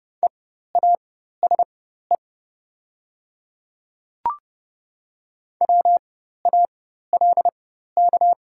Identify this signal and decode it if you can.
Morse code, heard as a keyed tone, about 30 wpm, spaced as for 14 wpm, 690 Hz. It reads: EASE WALK
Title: 4x Character Spacing - Top 300 Words - 30wpm